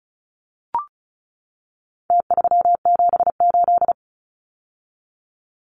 Morse code, heard as a keyed tone, about 34 wpm, 700 Hz. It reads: T378
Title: Numbers - Contest Style - 4-Digit 35wpm